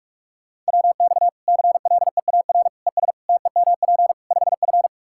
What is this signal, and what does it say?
Morse code, heard as a keyed tone, about 46 wpm, 700 Hz. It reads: WX CLEAR ES TEMP 5F
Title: Practice QSO - Exchange 2 - Info - 45wpm